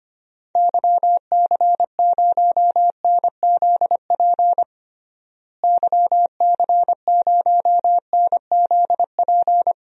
Morse code, heard as a keyed tone, about 25 wpm, 700 Hz. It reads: YC0NZP YC0NZP
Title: DXCC-100 Call Signs - 6 Characters - Repeated 2x - 25wpm